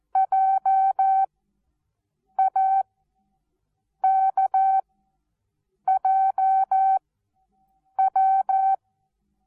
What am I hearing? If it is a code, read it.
Morse code, heard as a keyed tone, about 14 wpm, 770 Hz. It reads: J A K J W
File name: Onlinelehrgang / morsen / DJ4UF Morsen19